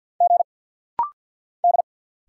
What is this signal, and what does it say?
Morse code, heard as a keyed tone, about 50 wpm, 690 Hz. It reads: G D
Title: Single Letters - Rapid-Fire 50wpm